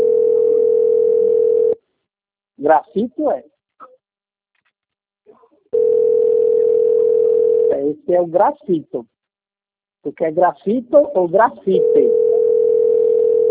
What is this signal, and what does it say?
Ringback tone (North America), 440 + 480 Hz, with speech.
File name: out- - - - - .8331